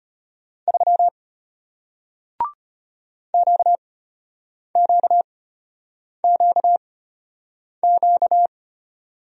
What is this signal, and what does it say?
Morse code, heard as a keyed tone, about 33 wpm, spaced as for 8 wpm, 700 Hz. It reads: 3 QQQQ